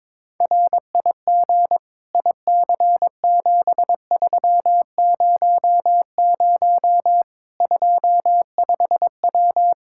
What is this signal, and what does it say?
Morse code, heard as a keyed tone, about 22 wpm, 700 Hz. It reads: RIG IC7300 25W
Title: K1USN SST QSO Practice 22wpm